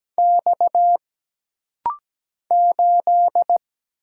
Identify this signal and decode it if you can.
Morse code, heard as a keyed tone, about 17 wpm, 700 Hz. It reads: X 8
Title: Single Letter-Number Alt - Rapid-Fire 17wpm